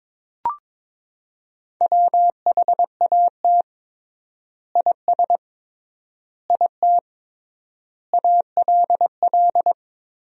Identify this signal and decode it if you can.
Morse code, heard as a keyed tone, about 22 wpm, 700 Hz. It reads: WHAT IS IT ALL
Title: 3x Word-Spacing - Sentences from Top 100 Words - 22wpm